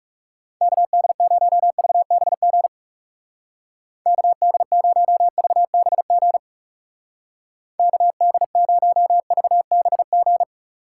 Morse code, heard as a keyed tone, about 38 wpm, 700 Hz. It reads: KD0VBG KD0VBG KD0VBG